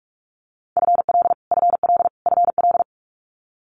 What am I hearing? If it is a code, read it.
Morse code, heard as a keyed tone, about 46 wpm, 700 Hz. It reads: FL FL FL
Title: Multi-letter Phonemes Repeated 3x 45wpm